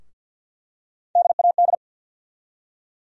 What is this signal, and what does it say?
Morse code, heard as a keyed tone, about 48 wpm, 700 Hz. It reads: DAD
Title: USA Call Sign Postfixes - 3 Letters 50wpm